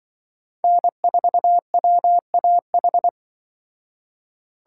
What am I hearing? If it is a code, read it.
Morse code, heard as a keyed tone, about 24 wpm, 700 Hz. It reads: N4WAH
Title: Speed Racing - USA 1x3 Call Signs 20wpm